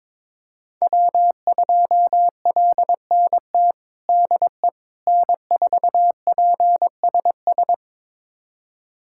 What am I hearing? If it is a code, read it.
Morse code, heard as a keyed tone, about 22 wpm, 700 Hz. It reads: W2LNT DE N4PSS
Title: USA Callsign DE Callsign - 5 Characters - 22wpm